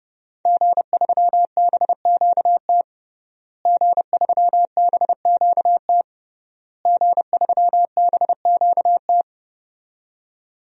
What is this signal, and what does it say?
Morse code, heard as a keyed tone, about 30 wpm, 700 Hz. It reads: G3BQT G3BQT G3BQT
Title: DXCC-100 Call Signs - 5 Characters - Repeated 3x - 30wpm